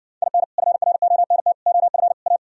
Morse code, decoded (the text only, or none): IN FRONT OF A